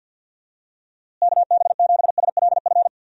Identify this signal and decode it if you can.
Morse code, heard as a keyed tone, about 50 wpm, 700 Hz. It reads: KB7SLF
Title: USA Call Signs - 6 Characters - Repeated 2x - 50wpm